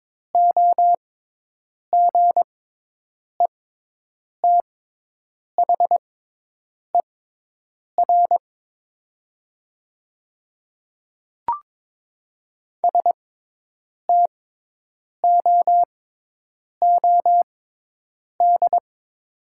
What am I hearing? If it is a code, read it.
Morse code, heard as a keyed tone, about 22 wpm, spaced as for 8 wpm, 700 Hz. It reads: OGETHER STOOD
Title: 6x Character Spacing - Top 500 Words - 22wpm